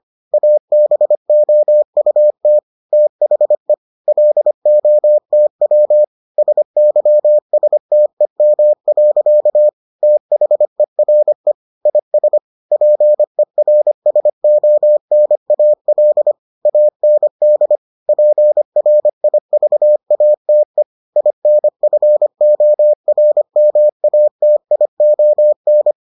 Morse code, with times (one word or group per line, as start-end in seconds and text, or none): 0.3-2.6 ABOUT
2.9-3.7 THE
4.1-6.0 LOTW
6.4-9.7 SYSTEM.
10.0-11.5 THERE
11.9-12.4 IS
12.7-16.3 PERSONAL
16.7-17.8 AND
18.1-20.8 PRIVATE
21.2-25.9 INFORMATION